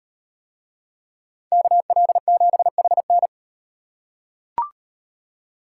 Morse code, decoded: KL7HN

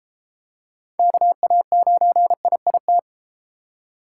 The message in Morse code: KA9IIT

